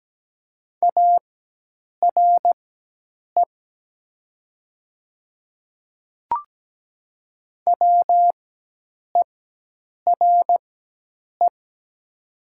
Morse code, heard as a keyed tone, about 17 wpm, spaced as for 8 wpm, 700 Hz. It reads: ARE WERE
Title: 4x Character Spacing - Top 100 Words - 17wpm